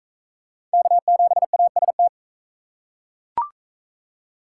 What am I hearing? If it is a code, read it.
Morse code, heard as a keyed tone, about 41 wpm, 700 Hz. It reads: K7AST